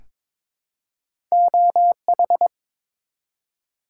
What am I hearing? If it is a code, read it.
Morse code, heard as a keyed tone, about 21 wpm, 700 Hz. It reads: OH
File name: ARRL Sections v2 - 22wpm